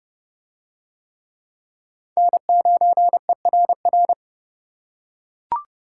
Morse code, heard as a keyed tone, about 29 wpm, 700 Hz. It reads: N9ERR